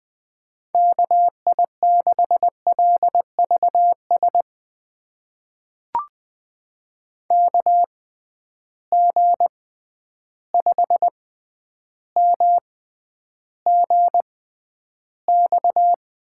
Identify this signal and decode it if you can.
Morse code, heard as a keyed tone, about 20 wpm, spaced as for 10 wpm, 700 Hz. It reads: KI6LVS KG5MGX
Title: USA Call Signs - 6 Characters - 6x Character Spacing - 20wpm